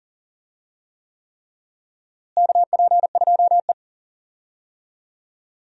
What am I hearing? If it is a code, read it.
Morse code, heard as a keyed tone, about 40 wpm, 700 Hz. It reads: KP2E
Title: DXCC-100 Call Signs - 4 Characters - Repeated 3x - 40wpm